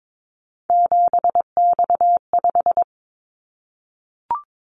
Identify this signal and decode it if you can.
Morse code, heard as a keyed tone, about 22 wpm, 700 Hz. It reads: 7X5
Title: Callsign Prefixes for Top 100 DXCC - 22wpm